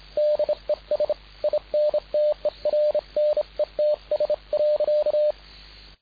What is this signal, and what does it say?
Morse code, heard as a keyed tone, about 29 wpm, 600 Hz. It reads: DES INTERNETS.
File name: dk5ke / BUG-Beispiel